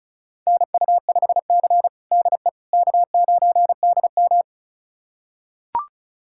Morse code, heard as a keyed tone, about 34 wpm, 700 Hz. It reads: NU5C DE K9DM
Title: USA Callsign DE Callsign - 4 Characters - 35wpm